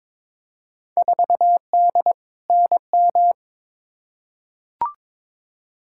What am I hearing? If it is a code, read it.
Morse code, heard as a keyed tone, about 22 wpm, 700 Hz. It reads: 4D NM
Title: Field Day Exchanges Encore - 22wpm